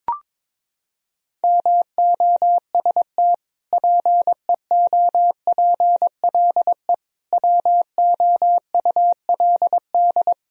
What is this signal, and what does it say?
Morse code, heard as a keyed tone, about 22 wpm, 700 Hz. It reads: MOST PEOPLE WOULD
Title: Sentences from Top 1000 Words - Review Part 1 - 22wpm